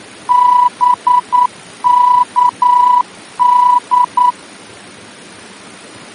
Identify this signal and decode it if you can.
Morse code, heard as a keyed tone, about 9 wpm, 980 Hz. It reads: BKD
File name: germany / mx / BKD-D-VOR